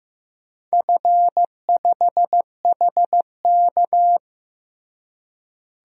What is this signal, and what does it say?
Morse code, heard as a keyed tone, about 15 wpm, 700 Hz. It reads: F5HK
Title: DXCC-100 Call Signs - 4 Characters - Repeated 2x - 15wpm